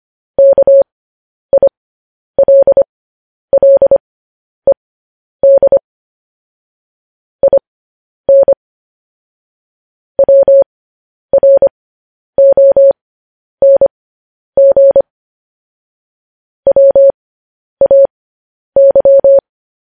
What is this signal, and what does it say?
Morse code, heard as a keyed tone, about 25 wpm, spaced as for 10 wpm, 550 Hz. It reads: KILLED IN WRONG WAY
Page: news headlines in morse code at 10 wpm